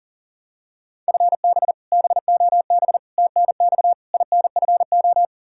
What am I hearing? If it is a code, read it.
Morse code, heard as a keyed tone, about 40 wpm, 700 Hz. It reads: FB BOB TNX INFO
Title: K1USN SST QSO Practice 40wpm